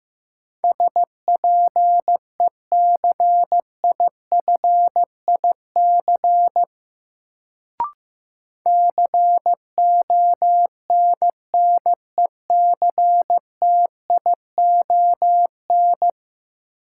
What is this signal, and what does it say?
Morse code, heard as a keyed tone, about 15 wpm, 700 Hz. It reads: SPECIFIC CONNECTION
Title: Sentences from Top 1300 Words - New Word Drill Alt - 15wpm